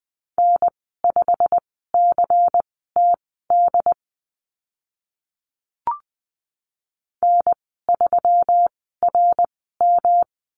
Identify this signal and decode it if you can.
Morse code, heard as a keyed tone, about 20 wpm, spaced as for 14 wpm, 700 Hz. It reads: N5CTD N3RM
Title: 154 Lesson 25g - Letter p - Call Signs 20wpm